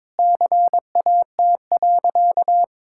Morse code, heard as a keyed tone, about 22 wpm, 700 Hz. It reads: CAT.
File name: practice / 22wpm / Sentences with 3-Word Phrases - 22wpm